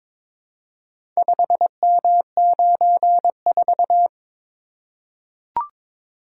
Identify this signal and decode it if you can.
Morse code, heard as a keyed tone, about 22 wpm, 700 Hz. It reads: 5M94
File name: practice / 22wpm / Four Letter-Number Combinations - 22wpm